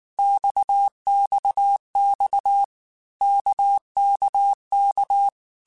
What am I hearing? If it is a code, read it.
Morse code, heard as a keyed tone, about 19 wpm, 800 Hz. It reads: XXX KKK
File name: Basic Letter Groups By7Groups / RKPX / RKPX@18